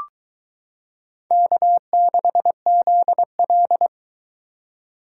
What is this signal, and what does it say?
Morse code, heard as a keyed tone, about 23 wpm, 700 Hz. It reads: K6ZL